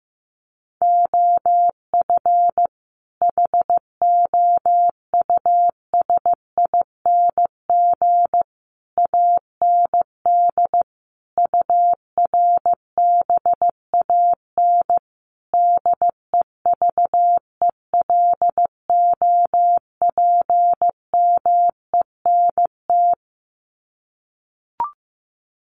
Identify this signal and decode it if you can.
Morse code, heard as a keyed tone, about 15 wpm, 700 Hz. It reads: OF HOUSING AND URBAN DEVELOPMENT
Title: Sets of 5 Words Encore2 15wpm